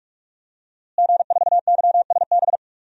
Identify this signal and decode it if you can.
Morse code, heard as a keyed tone, about 46 wpm, 700 Hz. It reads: G4YSB
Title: DXCC-100 Call Signs - 5 Characters - Repeated 3x - 45wpm